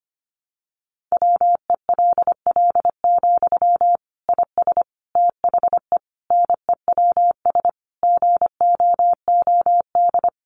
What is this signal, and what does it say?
Morse code, heard as a keyed tone, about 25 wpm, 700 Hz. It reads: WELL, IS THE NEWS GOOD
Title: Sentences from Top 800 Words 25wpm